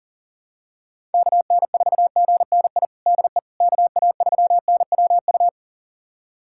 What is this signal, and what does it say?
Morse code, heard as a keyed tone, about 40 wpm, 700 Hz. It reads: KN4GNI DE KA3NWU
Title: USA Callsign DE Callsign - 6 Characters - 40wpm